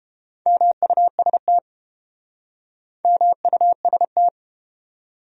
Morse code, heard as a keyed tone, about 31 wpm, 700 Hz. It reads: MUST MUST